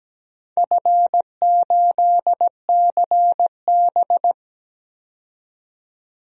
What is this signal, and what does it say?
Morse code, heard as a keyed tone, about 17 wpm, 700 Hz. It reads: F8CB
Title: DXCC-100 Call Signs - 4 Characters - 17wpm